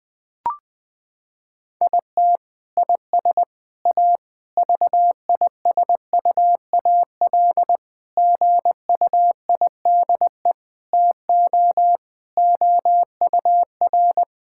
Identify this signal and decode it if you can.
Morse code, heard as a keyed tone, about 20 wpm, 700 Hz. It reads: IT IS A VISUAL GUIDE TO OUR
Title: Sentences from Top 1000 Words - Review Part 6 - 20wpm